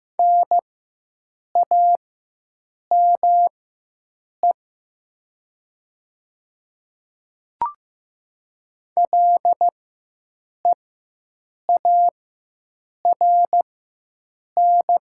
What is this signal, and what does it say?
Morse code, heard as a keyed tone, about 15 wpm, spaced as for 7 wpm, 690 Hz. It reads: NAME LEARN
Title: 4x Character Spacing - Top 500 Words - 15wpm